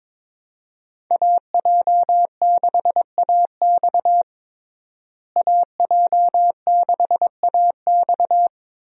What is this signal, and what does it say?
Morse code, heard as a keyed tone, about 22 wpm, 700 Hz. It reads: AJ6AX AJ6AX
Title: USA Call Signs - 5 Characters - Repeated 3x - 22wpm